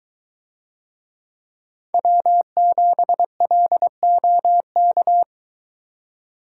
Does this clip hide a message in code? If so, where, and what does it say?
Yes: Morse code from 1.9 to 5.2 s, heard as a keyed tone, about 22 wpm, 700 Hz. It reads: W7LOK